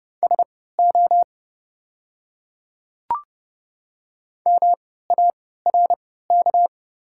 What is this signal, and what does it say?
Morse code, heard as a keyed tone, about 30 wpm, spaced as for 17 wpm, 690 Hz. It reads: SO MARK
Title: 3x Character Spacing - Top 300 Words - 30wpm